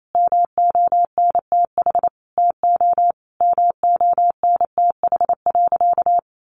Morse code, heard as a keyed tone, about 28 wpm, 700 Hz. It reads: MONTH TO MONTH.
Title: Sentences from Top 1200 Words 28wpm